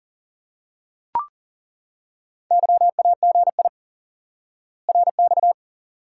Morse code, heard as a keyed tone, about 40 wpm, 700 Hz. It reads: YAGI RX
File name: course / instant-qso-element-course-3 / IQE Course 3 - Lesson 89 - YAGI - Two Elements - 40wpm